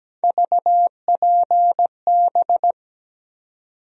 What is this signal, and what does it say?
Morse code, heard as a keyed tone, about 17 wpm, 700 Hz. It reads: VPB
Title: USA Call Sign Postfixes - 3 Letters 17wpm